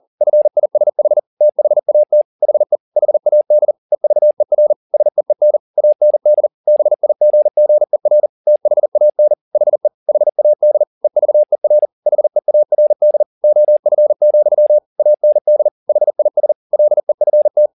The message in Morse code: FISH THAT HE HAD EVER SEEN AND BIGGER THAN HE HAD EVER HEARD OF, AND HIS LEFT

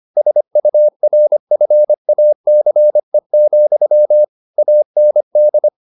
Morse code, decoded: SURFACE, AND

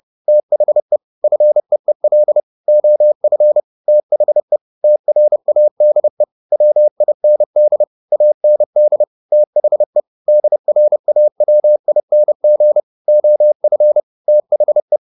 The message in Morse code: THE FEEL OF THE TRADE WIND AND THE DRAWING OF THE

